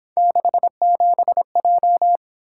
Morse code, heard as a keyed tone, about 26 wpm, 700 Hz. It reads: 67J
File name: Speed-Racing - Three Letter-Number Combinations - 22wpm